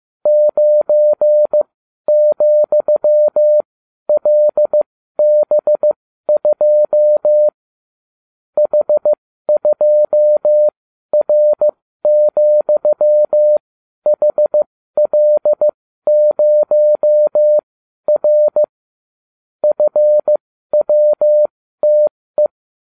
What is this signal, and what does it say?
Morse code, heard as a keyed tone, about 15 wpm, spaced as for 12 wpm, 600 Hz. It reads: 9,LB2 H2R,HL0R FWTE